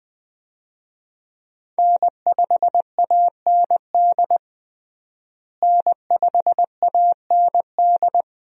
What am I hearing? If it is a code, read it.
Morse code, heard as a keyed tone, about 20 wpm, 700 Hz. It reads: N5AND N5AND
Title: USA Call Signs - 5 Characters - Repeated 2x - 20wpm